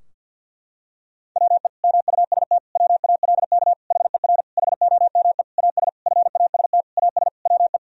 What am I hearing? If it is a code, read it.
Morse code, heard as a keyed tone, about 51 wpm, 700 Hz. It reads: WE MUST WALK HER HOME AS FAST AS WE
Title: Sentences from Top 1000 Words - Review Part 2 - 50wpm